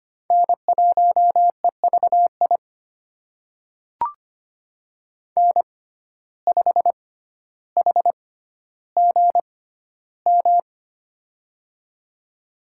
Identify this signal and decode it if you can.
Morse code, heard as a keyed tone, about 25 wpm, spaced as for 13 wpm, 700 Hz. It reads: N1EVI N5HGM
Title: USA Call Signs - 6x Character Spacing - 25wpm